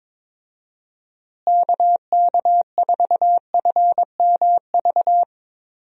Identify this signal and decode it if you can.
Morse code, heard as a keyed tone, about 22 wpm, 700 Hz. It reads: KK4FMV
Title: USA Call Signs - 6 Characters - Repeated 2x - 22wpm